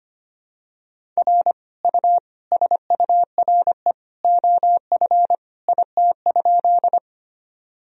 Morse code, heard as a keyed tone, about 25 wpm, 700 Hz. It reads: R U SURE OF IT?
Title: Sentences from Top 300 Words and Abbreviations 25wpm